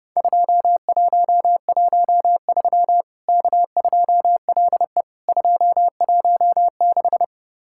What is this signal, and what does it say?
Morse code, heard as a keyed tone, about 30 wpm, 700 Hz. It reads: 2113 K2LE 216
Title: Worked All Europe DX Contest - QTC Exchange - 30wpm